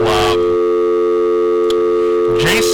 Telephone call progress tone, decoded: dial tone